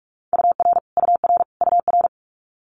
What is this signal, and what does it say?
Morse code, heard as a keyed tone, about 46 wpm, 700 Hz. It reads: UR UR UR
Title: Multi-letter Phonemes Repeated 3x 45wpm